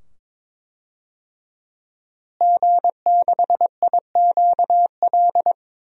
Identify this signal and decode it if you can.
Morse code, heard as a keyed tone, about 21 wpm, 700 Hz. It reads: G6IQL
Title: DXCC-100 Call Signs - 5 Characters - Repeated 2x - 22wpm